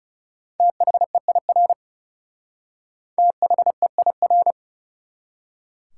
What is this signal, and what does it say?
Morse code, heard as a keyed tone, about 37 wpm, 700 Hz. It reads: THEIR THEIR
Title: Speed-Racing Top 180 QSO Elements 30wpm